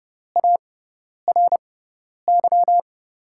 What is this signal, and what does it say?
Morse code, heard as a keyed tone, about 30 wpm, spaced as for 10 wpm, 700 Hz. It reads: ARY
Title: USA Call Signs - 5 Characters - 6x Character Spacing - 30wpm